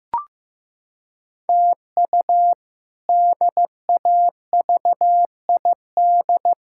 Morse code, heard as a keyed tone, about 15 wpm, 700 Hz. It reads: TU DAVID
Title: K1USN SST QSO Practice - 15wpm